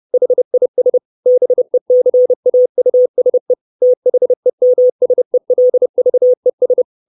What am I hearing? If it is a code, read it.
Morse code, heard as a keyed tone, about 30 wpm, 500 Hz. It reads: HIS BECAUSE THEMSELVES
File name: 632 Words 30WPM